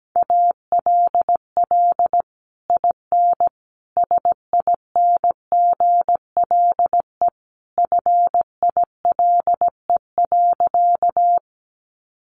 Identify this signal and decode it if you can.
Morse code, heard as a keyed tone, about 17 wpm, 700 Hz. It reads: ALL IN SINGLE FILE.
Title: Sentences from Top 1200 Words 17wpm